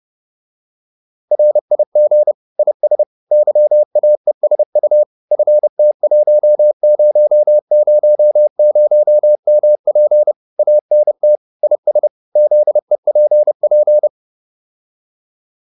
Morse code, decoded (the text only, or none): RIG IS YAESU FT1000MP ANT IS ZEPP